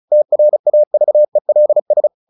Morse code, heard as a keyed tone, about 34 wpm, 600 Hz. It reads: TRAVELS
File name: the old man and the sea / the old man and the sea 1 35wpm